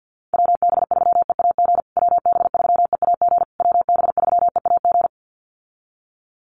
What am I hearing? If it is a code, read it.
Morse code, heard as a keyed tone, about 49 wpm, 700 Hz. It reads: WB3EAG WB3EAG WB3EAG
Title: USA Call Signs Repeated 3x 50wpm